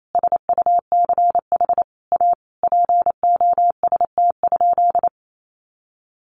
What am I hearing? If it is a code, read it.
Morse code, heard as a keyed tone, about 28 wpm, 700 Hz. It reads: SUCH A POST?
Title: Sentences from Top 1000 Words 28wpm